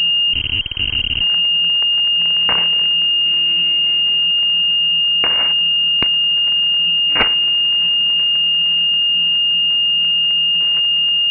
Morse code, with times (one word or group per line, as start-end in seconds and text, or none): none